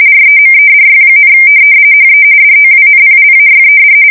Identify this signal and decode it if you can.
RTTY, about 45.45 baud. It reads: REMEDY, HE DOWNPLAYED
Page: 170 Shift 2125/2295